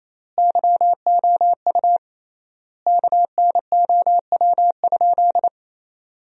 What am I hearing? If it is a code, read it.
Morse code, heard as a keyed tone, about 27 wpm, 700 Hz. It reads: YOU KNOW?
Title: 3x Word-Spacing - Sentences from Top 100 Words - 28wpm